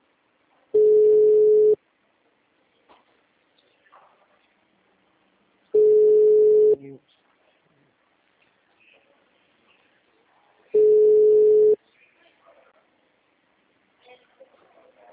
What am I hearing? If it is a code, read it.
Ringback tone (Europe, 425 Hz).